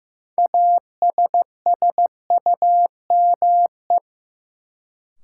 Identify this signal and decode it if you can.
Morse code, heard as a keyed tone, about 15 wpm, 700 Hz. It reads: ASSUME
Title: Sentences from Top 1500 Words - New Word Drill Alt - 15wpm